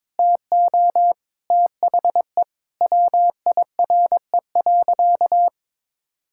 Morse code, heard as a keyed tone, about 22 wpm, 700 Hz. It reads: TO THE WIRE.
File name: practice / 22wpm / Sentences with 3-Word Phrases - 22wpm